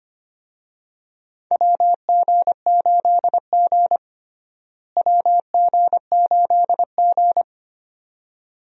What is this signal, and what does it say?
Morse code, heard as a keyed tone, about 25 wpm, 700 Hz. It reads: WG8G WG8G
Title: USA Call Signs - 4 Characters - Repeated 2x - 25wpm